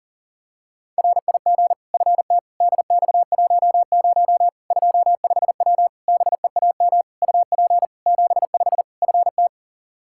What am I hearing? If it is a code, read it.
Morse code, heard as a keyed tone, about 40 wpm, 700 Hz. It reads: RIG FT DX10 25W BEAM UP 75 FT
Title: Practice QSO - Exchange 2 - Info - 40wpm